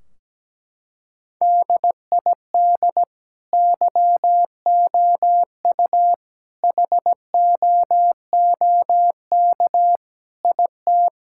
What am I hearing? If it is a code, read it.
Morse code, heard as a keyed tone, about 17 wpm, 700 Hz. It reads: DID YOU HOOK IT
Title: Sentences from Top 1600 Words 17wpm